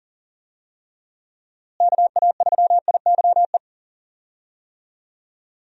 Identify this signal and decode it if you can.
Morse code, heard as a keyed tone, about 38 wpm, 700 Hz. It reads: KA3IYE
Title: USA Call Signs - 6 Characters - Repeated 2x - 40wpm